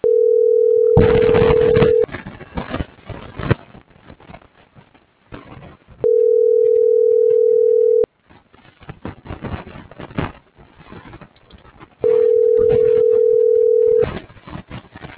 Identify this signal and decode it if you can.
Ringback tone (North America), 440 + 480 Hz.